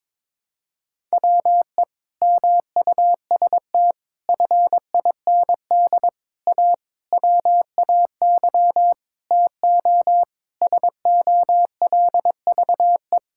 Morse code, heard as a keyed tone, about 22 wpm, 700 Hz. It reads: WE MUST FIND A WAY TO SOLVE
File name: practice / 22wpm / Sentences with 4-Word Phrases - 22wpm